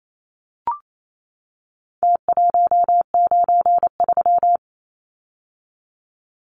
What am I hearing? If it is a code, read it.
Morse code, heard as a keyed tone, about 28 wpm, 700 Hz. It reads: T193
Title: Numbers - Contest Style - 4-Digit 28wpm